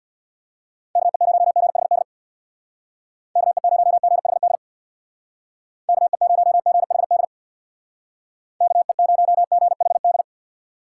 Morse code, heard as a keyed tone, about 62 wpm, 700 Hz. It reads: KE0GHD KE0GHD KE0GHD KE0GHD
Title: Speed Racing - USA 2x3 Call Signs 50wpm